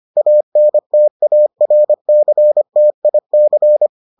Morse code, heard as a keyed tone, about 25 wpm, 600 Hz. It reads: ANTARCTIC